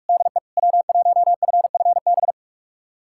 Morse code, heard as a keyed tone, about 41 wpm, 700 Hz. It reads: DE W1FFB
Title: USA Callsign DE Callsign - 5 Characters - 45wpm